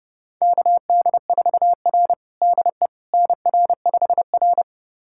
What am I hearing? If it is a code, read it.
Morse code, heard as a keyed tone, about 32 wpm, 700 Hz. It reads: KD4R DE NR5R
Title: USA Callsign DE Callsign - 4 Characters - 30wpm